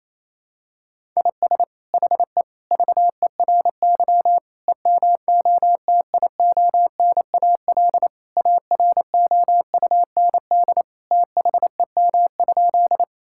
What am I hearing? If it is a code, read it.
Morse code, heard as a keyed tone, about 28 wpm, 700 Hz. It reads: IS HE VERY EMOTIONAL AROUND THEM?
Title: Sentences from Top 1000 Words - Review Part 5 - 28wpm